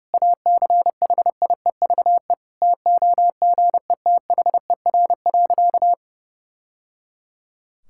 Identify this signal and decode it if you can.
Morse code, heard as a keyed tone, about 32 wpm, 700 Hz. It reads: ACHIEVE TOGETHER.
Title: Sentences from Top 1000 Words - Review Part 6 - 30wpm